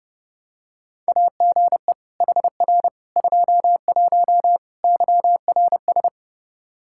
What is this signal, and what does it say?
Morse code, heard as a keyed tone, about 32 wpm, 700 Hz. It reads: AGE HR 21 YRS
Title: K1USN SST QSO Practice 30wpm